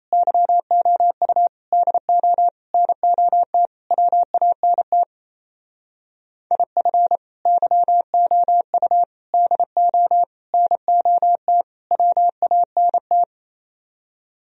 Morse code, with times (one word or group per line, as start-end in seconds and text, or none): 0.1-1.5 YOU
1.7-2.5 DO
2.7-3.7 NOT
3.9-5.0 WANT
6.5-7.2 IF
7.5-9.0 YOU
9.3-10.2 DO
10.5-11.6 NOT
11.9-13.2 WANT